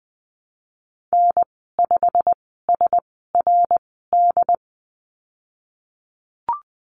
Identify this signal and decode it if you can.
Morse code, heard as a keyed tone, about 20 wpm, spaced as for 15 wpm, 700 Hz. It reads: N5SRD
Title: 173 Lesson 28g - Number 9 - Call Signs 20wpm